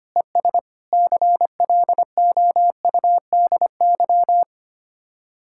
Morse code, heard as a keyed tone, about 25 wpm, 700 Hz. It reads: ES CLOUDY
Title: Speed-Racing Common QSO Phrases - 25wpm